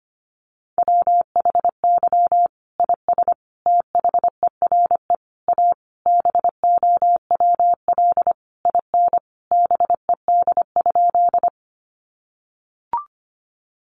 Morse code, heard as a keyed tone, about 25 wpm, 700 Hz. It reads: WHY IS THERE A BOWL IN BED?
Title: Sentences from Top 1200 Words 25wpm